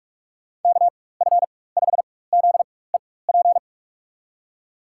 Morse code, heard as a keyed tone, about 42 wpm, spaced as for 21 wpm, 700 Hz. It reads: KF5ZEP
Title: USA Call Signs - 6 Characters - 4x Character Spacing - 45wpm